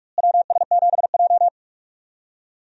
Morse code, heard as a keyed tone, about 44 wpm, 700 Hz. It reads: WS7J
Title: USA Call Signs - 4 Characters - Repeated 3x - 45wpm